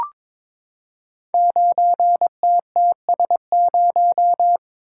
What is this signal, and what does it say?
Morse code, heard as a keyed tone, about 22 wpm, 700 Hz. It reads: 9TTS0
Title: Code Groups - 5 Letter:Numbers - 22wpm